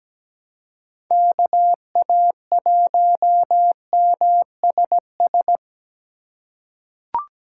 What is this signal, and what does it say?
Morse code, heard as a keyed tone, about 17 wpm, 700 Hz. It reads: KA1MSS
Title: USA Call Signs - 6x Character Spacing - 17wpm